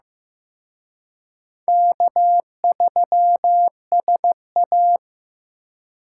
Morse code, heard as a keyed tone, about 18 wpm, 700 Hz. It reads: K3SA
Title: Speed Racing - USA 1x2 Call Signs 15wpm